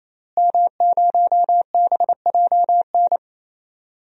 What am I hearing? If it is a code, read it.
Morse code, heard as a keyed tone, about 27 wpm, 700 Hz. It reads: M0BJN
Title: DXCC-100 Call Signs - 5 Characters - Repeated 3x - 28wpm